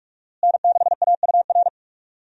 Morse code, heard as a keyed tone, about 42 wpm, 700 Hz. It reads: N6AUR
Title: USA Call Signs - 5 Characters - Repeated 3x - 45wpm